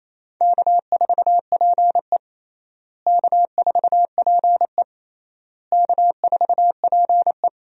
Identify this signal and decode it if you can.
Morse code, heard as a keyed tone, about 28 wpm, 700 Hz. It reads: K4PE K4PE K4PE
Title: USA Call Signs - 4 Characters - Repeated 3x - 28wpm